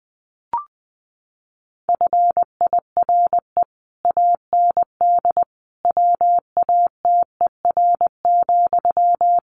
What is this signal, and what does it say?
Morse code, heard as a keyed tone, about 20 wpm, 700 Hz. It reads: FIRE AND WATER,
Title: Sentences from Top 400 Words 20wpm